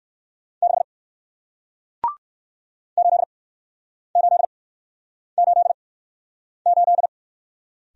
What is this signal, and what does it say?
Morse code, heard as a keyed tone, about 60 wpm, spaced as for 9 wpm, 700 Hz. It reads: 6 8888